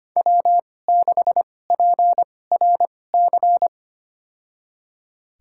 Morse code, heard as a keyed tone, about 25 wpm, spaced as for 18 wpm, 700 Hz. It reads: W6PRC
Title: USA Call Signs - 2x Character Spacing - 25wpm